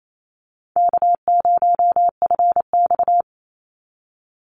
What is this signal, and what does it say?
Morse code, heard as a keyed tone, about 28 wpm, 700 Hz. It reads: K0FX
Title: Call Signs - 28wpm